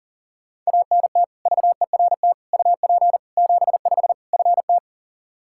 Morse code, heard as a keyed tone, about 40 wpm, 700 Hz. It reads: ANT VERT UP 75 FT
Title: K1USN SST QSO Practice 40wpm